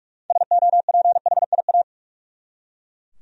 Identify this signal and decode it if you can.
Morse code, heard as a keyed tone, about 43 wpm, 700 Hz. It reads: SOPHIA
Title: USA Common Names - 45wpm